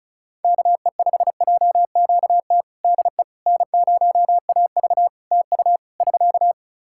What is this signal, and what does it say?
Morse code, heard as a keyed tone, about 36 wpm, 700 Hz. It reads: KE5JQT DE N0AV TU <SK>
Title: Practice QSO - Exchange 4 - Ending - 35wpm